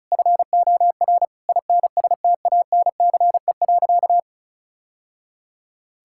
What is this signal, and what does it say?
Morse code, heard as a keyed tone, about 36 wpm, 700 Hz. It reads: FOR INSTANCE.
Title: Sentences from Top 1000 Words - Review Part 5 - 35wpm